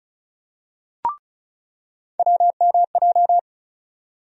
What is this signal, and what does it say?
Morse code, heard as a keyed tone, about 36 wpm, 700 Hz. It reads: WMJ
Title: USA Call Sign Postfixes - 3 Letters 35wpm